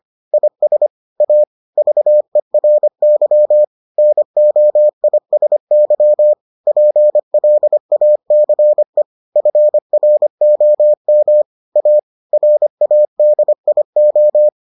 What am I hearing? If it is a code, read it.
Morse code, heard as a keyed tone, about 25 wpm, 600 Hz. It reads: IS A VERY NOISY PLACE FROM A RADIO